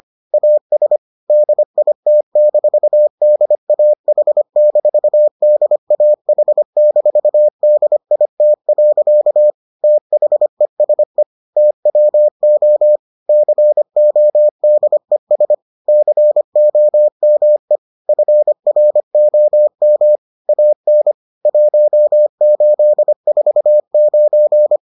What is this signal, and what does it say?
Morse code, heard as a keyed tone, about 25 wpm, 600 Hz. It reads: AS DIT-DAH-DAH-DIT. THESE TWO CODES COME FROM AN 1849